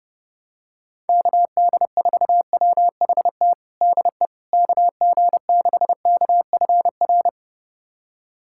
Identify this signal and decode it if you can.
Morse code, heard as a keyed tone, about 30 wpm, 700 Hz. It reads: KD4WHT DE KG6KFR
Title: USA Callsign DE Callsign - 6 Characters - 30wpm